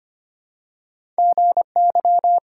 Morse code, heard as a keyed tone, about 25 wpm, 700 Hz. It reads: GY
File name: Two Letter-Number Combinations - 25wpm